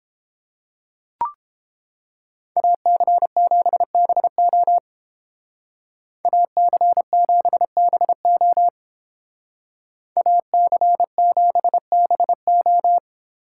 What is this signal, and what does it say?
Morse code, heard as a keyed tone, about 30 wpm, 700 Hz. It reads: AC7BO AC7BO AC7BO